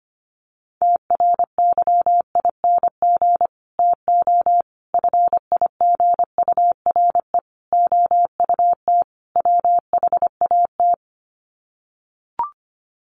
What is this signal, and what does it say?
Morse code, heard as a keyed tone, about 25 wpm, 700 Hz. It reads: TRYING TO FIGURE OUT WHAT
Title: Sets of 5 Words 25wpm